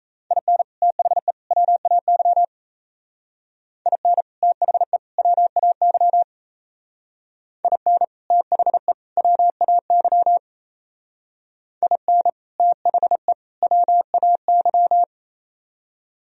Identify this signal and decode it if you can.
Morse code, heard as a keyed tone, about 33 wpm, 700 Hz. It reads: IN THE WAY IN THE WAY IN THE WAY IN THE WAY
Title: Speed-Racing - Sets of 3 Words 28wpm